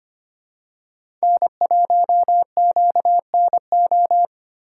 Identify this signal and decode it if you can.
Morse code, heard as a keyed tone, about 25 wpm, 700 Hz. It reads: N1QNO
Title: USA Call Signs - 6x Character Spacing - 25wpm